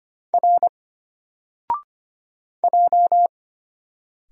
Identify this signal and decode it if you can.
Morse code, heard as a keyed tone, about 25 wpm, 700 Hz. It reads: R J
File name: Single-Letter-Number-25wpm